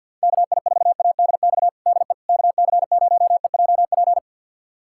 Morse code, heard as a keyed tone, about 47 wpm, 700 Hz. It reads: KI4ADX DE KC0EJP
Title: USA Callsign DE Callsign - 6 Characters - 50wpm